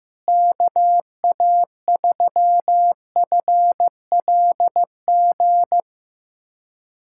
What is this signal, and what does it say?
Morse code, heard as a keyed tone, about 15 wpm, 700 Hz. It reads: KA3FLG
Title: USA Callsign DE Callsign - 6 Characters - 15wpm